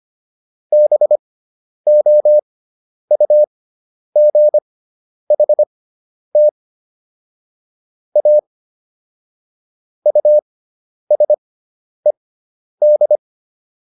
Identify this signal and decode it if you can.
Morse code, heard as a keyed tone, about 25 wpm, spaced as for 10 wpm, 600 Hz. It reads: BOUGHT A USED